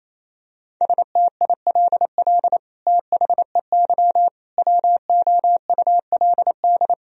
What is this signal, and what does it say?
Morse code, heard as a keyed tone, about 28 wpm, 700 Hz. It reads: STILL THEY WOULD
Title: Sentences from Top 1000 Words - Review Part 1 - 28wpm